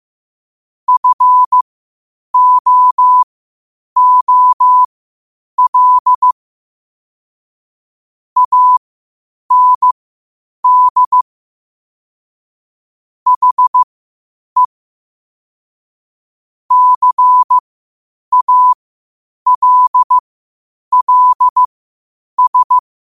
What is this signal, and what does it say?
Morse code, heard as a keyed tone, about 15 wpm, spaced as for 8 wpm, 1000 Hz. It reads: FOOL AND HE CALLS
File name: 8wpm